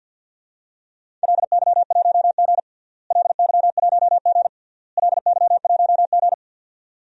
Morse code, heard as a keyed tone, about 47 wpm, 700 Hz. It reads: LY1G LY1G LY1G
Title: DXCC-100 Call Signs - 4 Characters - Repeated 3x - 50wpm